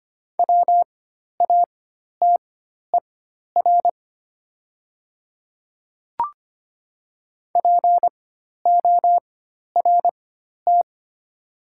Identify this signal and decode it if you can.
Morse code, heard as a keyed tone, about 25 wpm, spaced as for 12 wpm, 690 Hz. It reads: WATER PORT
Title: 4x Character Spacing - Top 200 Words - 25wpm